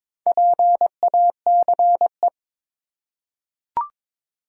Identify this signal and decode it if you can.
Morse code, heard as a keyed tone, about 22 wpm, 700 Hz. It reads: PACE
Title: Four Letter Words - 22wpm